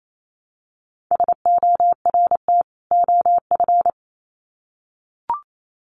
Morse code, heard as a keyed tone, about 28 wpm, 700 Hz. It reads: SORT OF
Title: Sets of 2 Words 28wpm